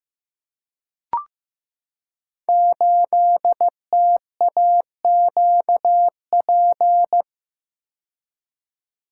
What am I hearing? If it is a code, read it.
Morse code, heard as a keyed tone, about 15 wpm, 700 Hz. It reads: 8TAQP
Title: Code Groups - 5 Letter:Numbers - 15wpm